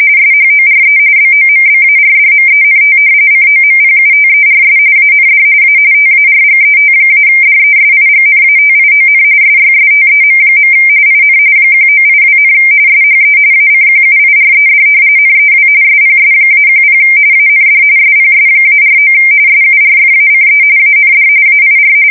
RTTY, about 45.45 baud. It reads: SHORE BOMBARDMENT COMPUTERS AND A TARGET-DESIGNATION SYSTEM. IN 1982, SHE ACQUIRED NEW SPS-10 AND SPS-49 SURFACE AND AIR
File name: fsk170 / battleshipnj